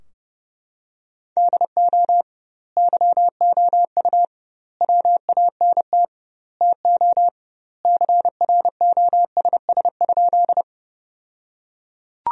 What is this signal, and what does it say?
Morse code, heard as a keyed tone, about 29 wpm, 700 Hz. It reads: DO YOU WANT TO CROSS?
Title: 2x Word-Spacing - Sentences from Top 300 Words - 30wpm